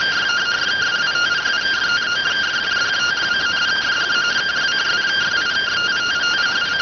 Two-tone teleprinter signal, about 170 Hz shift.